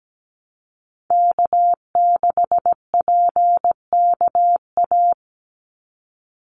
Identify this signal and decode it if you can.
Morse code, heard as a keyed tone, about 17 wpm, 700 Hz. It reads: K6PKA
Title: USA Call Signs 17wpm